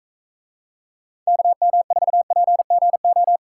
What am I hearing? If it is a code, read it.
Morse code, heard as a keyed tone, about 42 wpm, 700 Hz. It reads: KM4PGO